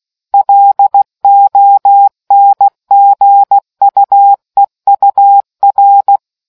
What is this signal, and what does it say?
Morse code, heard as a keyed tone, about 16 wpm, 800 Hz. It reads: LONGUEUR